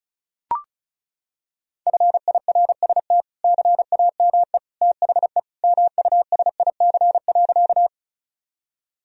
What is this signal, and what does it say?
Morse code, heard as a keyed tone, about 36 wpm, 700 Hz. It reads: FIRST CAME THE MUSIC.
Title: Sentences from Top 1000 Words - Review Part 1 - 35wpm